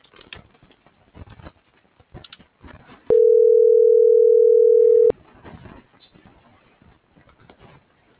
Ringback tone (North America), 440 + 480 Hz.